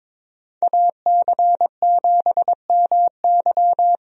Morse code, heard as a keyed tone, about 23 wpm, 700 Hz. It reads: AC7MY
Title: USA Call Signs - 5 Characters - 22wpm